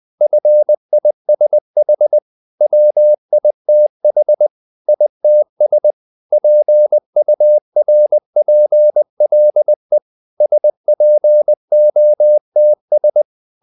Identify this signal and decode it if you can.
Morse code, heard as a keyed tone, about 20 wpm, 600 Hz. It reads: FISH WITH ITS PURPLE SPOTS
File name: the old man and the sea part 2 20wpm